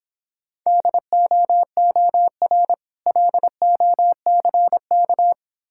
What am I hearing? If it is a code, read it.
Morse code, heard as a keyed tone, about 25 wpm, 700 Hz. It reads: DOOR LOCK